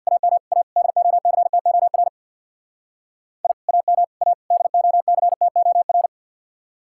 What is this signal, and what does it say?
Morse code, heard as a keyed tone, about 64 wpm, 700 Hz. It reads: AM A DOCTOR I AM A DOCTOR